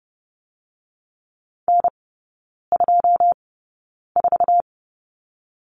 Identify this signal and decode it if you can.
Morse code, heard as a keyed tone, about 30 wpm, spaced as for 9 wpm, 700 Hz. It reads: N24
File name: 081 Lesson 14e - Number 2 - Three Characters 30wpm